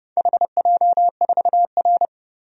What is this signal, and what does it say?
Morse code, heard as a keyed tone, about 32 wpm, 700 Hz. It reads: HJ4R
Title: DXCC-100 Call Signs - 4 Characters - 30wpm